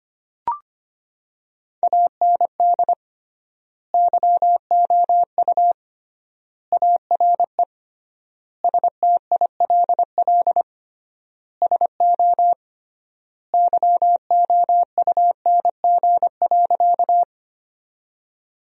Morse code, heard as a keyed tone, about 25 wpm, 700 Hz. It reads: AND YOU ARE STILL SO YOUNG.
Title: 3x Word-Spacing - Sentences from Top 400 Words - 25wpm